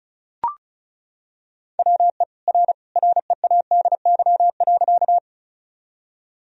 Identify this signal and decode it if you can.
Morse code, heard as a keyed tone, about 36 wpm, 700 Hz. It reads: WE R READY.
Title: Sentences from Top 400 Words and Abbreviations 35wpm